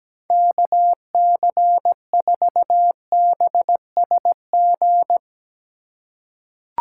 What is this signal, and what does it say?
Morse code, heard as a keyed tone, about 17 wpm, 700 Hz. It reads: KC4BSG
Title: USA Call Signs - 6 Characters - 4x Character Spacing - 17wpm